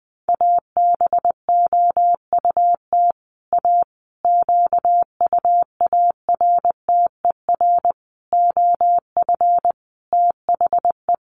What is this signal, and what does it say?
Morse code, heard as a keyed tone, about 20 wpm, 700 Hz. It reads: ABOUT A QUARTER OF THE
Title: Sentences from Top 1300 Words 20wpm